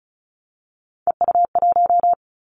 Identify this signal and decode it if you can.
Morse code, heard as a keyed tone, about 36 wpm, 700 Hz. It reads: EU1
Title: Callsign Prefixes for Top 50 DXCC - 35wpm